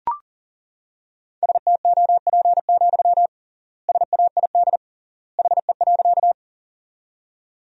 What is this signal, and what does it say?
Morse code, heard as a keyed tone, about 40 wpm, 700 Hz. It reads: STOP, SAID HE.
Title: 3x Word-Spacing - Sentences from Top 300 Words - 40wpm